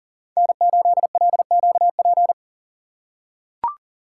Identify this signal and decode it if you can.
Morse code, heard as a keyed tone, about 40 wpm, 700 Hz. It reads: N8LQP
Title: USA Call Signs - 5 Characters - 3x Character Spacing - 40wpm